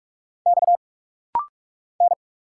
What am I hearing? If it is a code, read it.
Morse code, heard as a keyed tone, about 46 wpm, 690 Hz. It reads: X N
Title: Single Letters - Rapid-Fire 45wpm